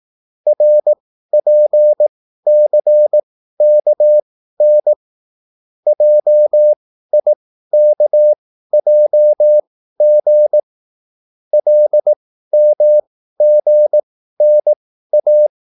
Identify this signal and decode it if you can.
Morse code, heard as a keyed tone, about 18 wpm, spaced as for 13 wpm, 600 Hz. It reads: RPCKN JIKJG LMGNA